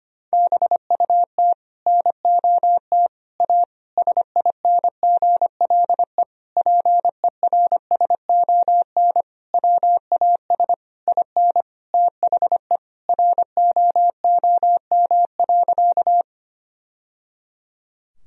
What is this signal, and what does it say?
Morse code, heard as a keyed tone, about 25 wpm, 690 Hz. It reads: BUT NOT A SINGLE PERSON WAS IN THE ROOM.
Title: Sentences from Top 1000 Words - Review Part 4 - 25wpm